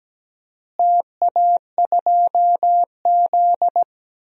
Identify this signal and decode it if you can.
Morse code, heard as a keyed tone, about 17 wpm, 700 Hz. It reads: TA2Z
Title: DXCC-100 Call Signs - 4 Characters - Repeated 3x - 17wpm